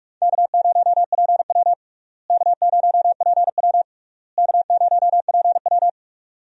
Morse code, heard as a keyed tone, about 42 wpm, 700 Hz. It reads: K0PW K0PW K0PW
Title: USA Call Signs - 4 Characters - Repeated 3x - 45wpm